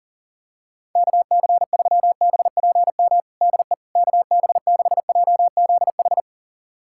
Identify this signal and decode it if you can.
Morse code, heard as a keyed tone, about 40 wpm, 700 Hz. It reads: KC3BPM DE KB6JZH
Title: USA Callsign DE Callsign - 6 Characters - 40wpm